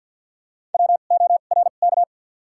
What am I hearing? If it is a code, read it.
Morse code, heard as a keyed tone, about 52 wpm, spaced as for 36 wpm, 700 Hz. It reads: WORK
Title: 2x Character Spacing - Top 300 Words - 50wpm